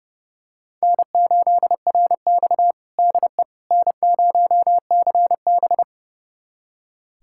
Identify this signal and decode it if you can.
Morse code, heard as a keyed tone, about 30 wpm, 700 Hz. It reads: N8RX DE N0CB
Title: USA Callsign DE Callsign - 4 Characters - 30wpm